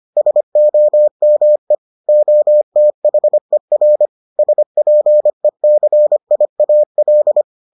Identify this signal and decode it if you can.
Morse code, heard as a keyed tone, about 25 wpm, 600 Hz. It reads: SOME OTHER SPECIAL